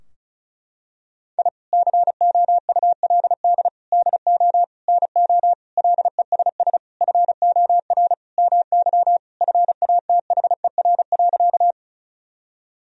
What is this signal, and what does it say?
Morse code, heard as a keyed tone, about 36 wpm, 700 Hz. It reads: I COULD DO NO LESS FOR MY FATHER.
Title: Sentences from Top 1000 Words - Review Part 2 - 35wpm